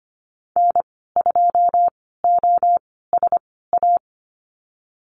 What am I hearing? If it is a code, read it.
Morse code, heard as a keyed tone, about 25 wpm, spaced as for 16 wpm, 700 Hz. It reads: N2OSA
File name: Lesson 15g - Number 5 - Call Signs 25wpm